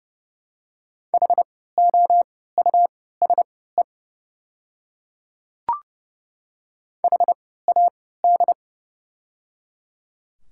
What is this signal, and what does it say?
Morse code, heard as a keyed tone, about 31 wpm, spaced as for 17 wpm, 700 Hz. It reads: HOUSE HAD
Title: 3x Character Spacing - Top 300 Words - 30wpm